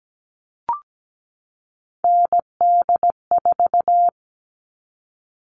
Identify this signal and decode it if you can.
Morse code, heard as a keyed tone, about 17 wpm, 700 Hz. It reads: ND4
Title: All Cut Numbers - 3-Digit 17wpm